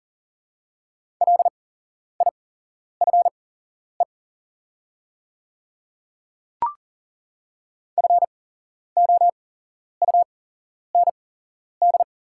Morse code, heard as a keyed tone, about 40 wpm, spaced as for 11 wpm, 700 Hz. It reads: LIFE FOUND
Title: 8x Character Spacing - Top 300 Words - 40wpm